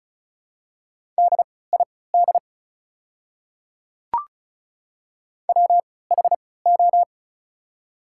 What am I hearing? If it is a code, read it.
Morse code, heard as a keyed tone, about 36 wpm, spaced as for 20 wpm, 690 Hz. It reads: DID WHO
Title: 3x Character Spacing - Top 100 Words - 35wpm